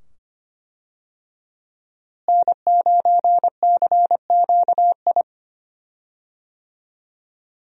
Morse code, heard as a keyed tone, about 25 wpm, 700 Hz. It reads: N9CQI
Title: USA Call Signs - 5 Characters - Repeated 2x - 25wpm